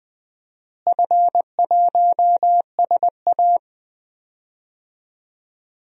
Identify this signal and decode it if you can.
Morse code, heard as a keyed tone, about 20 wpm, 700 Hz. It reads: F1SA
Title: DXCC-100 Call Signs - 4 Characters - Repeated 3x - 20wpm